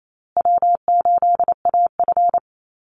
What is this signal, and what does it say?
Morse code, heard as a keyed tone, about 28 wpm, 700 Hz. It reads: W8AF
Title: USA Call Signs Repeated 2x 28wpm